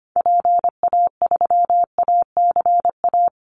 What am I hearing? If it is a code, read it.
Morse code, heard as a keyed tone, about 25 wpm, 700 Hz. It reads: PA3ACA
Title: Call Signs Repeated 3x - 25wpm